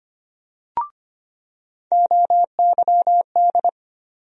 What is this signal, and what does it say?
Morse code, heard as a keyed tone, about 25 wpm, 700 Hz. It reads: OYD
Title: USA Call Sign Postfixes - 3 Letters 25wpm